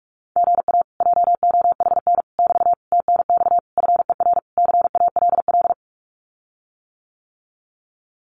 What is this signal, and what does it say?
Morse code, heard as a keyed tone, about 44 wpm, 700 Hz. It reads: GA JOHN <BT> TNX FER CALL
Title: Common QSO Phrases Encore 45wpm